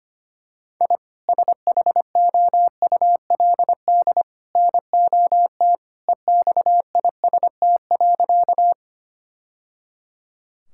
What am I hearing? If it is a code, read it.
Morse code, heard as a keyed tone, about 25 wpm, 700 Hz. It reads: I SHOULD NOT EXIST.
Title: Sentences from Top 1000 Words - Review Part 5 - 25wpm